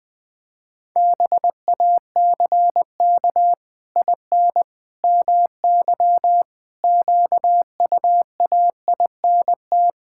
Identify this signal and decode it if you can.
Morse code, heard as a keyed tone, about 20 wpm, 700 Hz. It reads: BACK IN MY QUAINT
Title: Pangram Challenge v2 - 20wpm